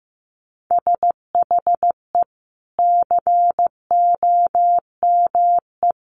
Morse code, heard as a keyed tone, about 15 wpm, 700 Hz. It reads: SHE COME
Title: Sentences from Top 1100 Words 15wpm